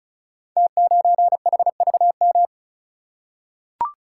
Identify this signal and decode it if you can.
Morse code, heard as a keyed tone, about 36 wpm, 700 Hz. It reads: T9HVM
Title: Code Groups - 5 Letter:Numbers - 35wpm